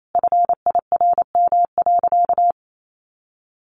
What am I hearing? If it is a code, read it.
Morse code, heard as a keyed tone, about 28 wpm, 700 Hz. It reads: FIRM.
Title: Sentences from Top 1400 Words 28wpm